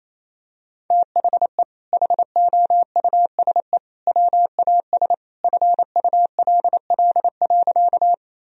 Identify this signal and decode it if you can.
Morse code, heard as a keyed tone, about 27 wpm, 700 Hz. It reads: THE HOUSE WAS FULL.
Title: Sentences from Top 1000 Words - Review Part 3 - 28wpm